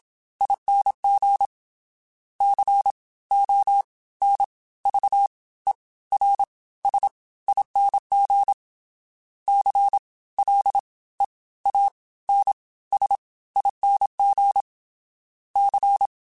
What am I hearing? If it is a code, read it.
Morse code, heard as a keyed tone, about 27 wpm, spaced as for 17 wpm, 800 Hz. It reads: ING CONVERSING CLEANSING C
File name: Special INGs / Special INGs-mp3s / Fast ing @25